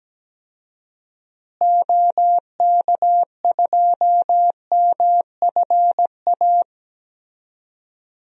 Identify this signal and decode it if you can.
Morse code, heard as a keyed tone, about 17 wpm, 700 Hz. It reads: OK2MFA
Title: DXCC-100 Call Signs - 6 Characters - Repeated 3x - 17wpm